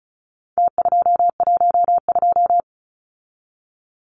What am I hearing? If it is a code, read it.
Morse code, heard as a keyed tone, about 34 wpm, 700 Hz. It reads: T212
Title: Numbers - Contest Style - 4-Digit 35wpm